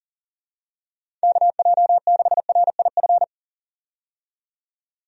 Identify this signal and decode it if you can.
Morse code, heard as a keyed tone, about 39 wpm, 700 Hz. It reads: KJ6RIF
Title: USA Call Signs - 6 Characters - 40wpm